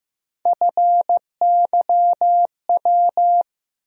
Morse code, heard as a keyed tone, about 15 wpm, 700 Hz. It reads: FYW
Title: USA Call Sign Postfixes - 3 Letters 15wpm